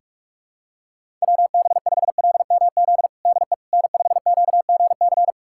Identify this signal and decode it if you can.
Morse code, heard as a keyed tone, about 45 wpm, 700 Hz. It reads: WB5LMZ DE N5QGC
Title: USA Callsign DE Callsign 45wpm